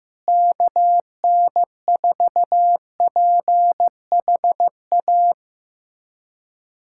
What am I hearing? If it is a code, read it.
Morse code, heard as a keyed tone, about 15 wpm, 700 Hz. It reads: KN4PHA
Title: USA Callsign DE Callsign 15wpm